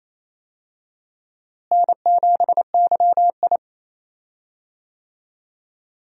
Morse code, heard as a keyed tone, about 28 wpm, 700 Hz. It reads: N7YI
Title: USA Call Signs - 4 Characters - Repeated 2x - 28wpm